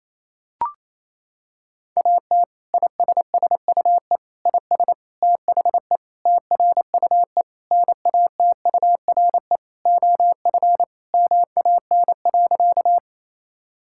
Morse code, heard as a keyed tone, about 27 wpm, 700 Hz. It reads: AT ISSUE IS THE TRUE NATURE OF MAN.
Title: Sentences from Top 1000 Words - Review Part 6 - 28wpm